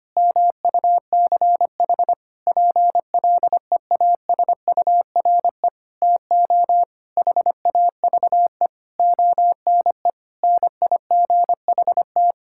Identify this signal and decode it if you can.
Morse code, heard as a keyed tone, about 25 wpm, 700 Hz. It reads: MUCH PLEASURE TO HAVE ONE NIGHT
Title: Sentences from Top 1600 Words 25wpm